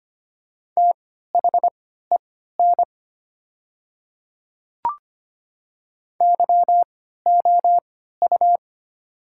Morse code, heard as a keyed tone, about 25 wpm, spaced as for 14 wpm, 700 Hz. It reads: THEN YOU
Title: 3x Character Spacing - Top 100 Words - 25wpm